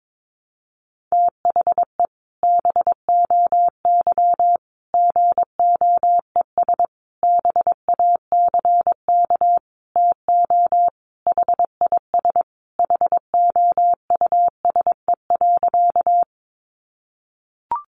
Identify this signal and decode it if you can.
Morse code, heard as a keyed tone, about 22 wpm, 700 Hz. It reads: THE BOY GOES BACK TO HIS HOUSE.
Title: Easy Phrases 22wpm